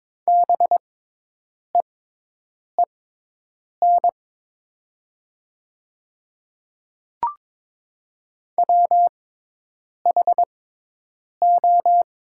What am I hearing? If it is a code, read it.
Morse code, heard as a keyed tone, about 22 wpm, spaced as for 8 wpm, 700 Hz. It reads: BEEN WHO
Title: 6x Character Spacing - Top 100 Words - 22wpm